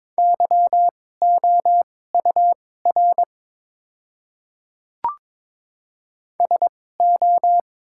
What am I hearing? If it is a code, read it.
Morse code, heard as a keyed tone, about 22 wpm, spaced as for 16 wpm, 700 Hz. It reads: YOUR SO
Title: 2x Character Spacing - Top 100 Words - 22wpm